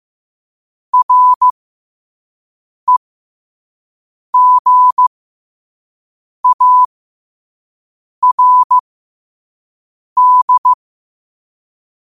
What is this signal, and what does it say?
Morse code, heard as a keyed tone, about 15 wpm, spaced as for 5 wpm, 1000 Hz. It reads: REGARD
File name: qotdfiles / 5wpm